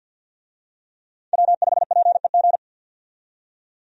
Morse code, heard as a keyed tone, about 49 wpm, 700 Hz. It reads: W5PEG
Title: USA Call Signs - 5 Characters - 50wpm